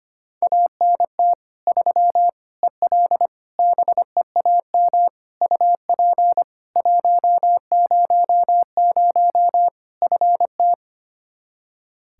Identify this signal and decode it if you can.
Morse code, heard as a keyed tone, about 25 wpm, 700 Hz. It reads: ANT 3 EL BEAM UP 100 FT
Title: Practice QSO - Exchange 2 - Info - 25wpm